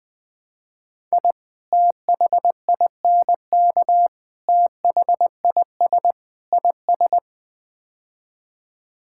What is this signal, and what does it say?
Morse code, heard as a keyed tone, about 20 wpm, 700 Hz. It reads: I THINK THIS IS
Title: Build A Phrase - 5-Word Phrases - 20wpm